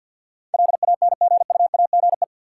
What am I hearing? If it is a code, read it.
Morse code, heard as a keyed tone, about 53 wpm, 700 Hz. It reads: LANGUAGE
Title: Sentences from Top 500 Words - New Word Drill - 50wpm